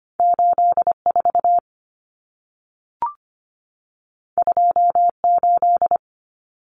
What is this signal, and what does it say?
Morse code, heard as a keyed tone, about 25 wpm, 700 Hz. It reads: 84 28
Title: Numbers - 2-Digit 25wpm